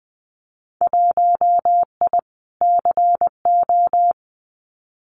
Morse code, heard as a keyed tone, about 20 wpm, 700 Hz. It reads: 1I CO
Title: Winter Field Day Exchanges Encore 20wpm